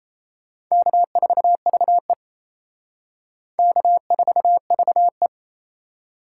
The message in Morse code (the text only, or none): K4VE K4VE